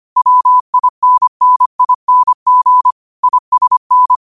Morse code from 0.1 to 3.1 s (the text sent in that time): WINNING